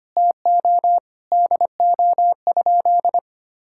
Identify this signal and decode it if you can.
Morse code, heard as a keyed tone, about 25 wpm, 700 Hz. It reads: TO DO?
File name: practice / 25wpm / Sentences with 2-Word Phrases - 25wpm